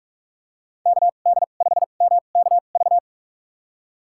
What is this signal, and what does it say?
Morse code, heard as a keyed tone, about 45 wpm, spaced as for 32 wpm, 700 Hz. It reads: KD5MKV
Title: USA Call Signs - 6 Characters - 2x Character Spacing - 45wpm